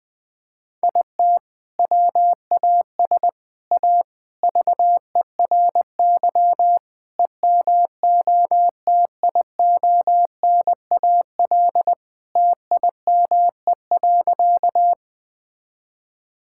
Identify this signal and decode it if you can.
Morse code, heard as a keyed tone, about 20 wpm, 700 Hz. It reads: IT WAS A VERY EMOTIONAL TIME.
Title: Sentences from Top 1000 Words - Review Part 5 - 20wpm